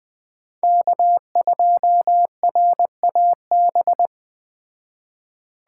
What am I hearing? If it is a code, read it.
Morse code, heard as a keyed tone, about 20 wpm, 700 Hz. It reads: K2RAB
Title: Speed Racing - USA 1x3 Call Signs 17wpm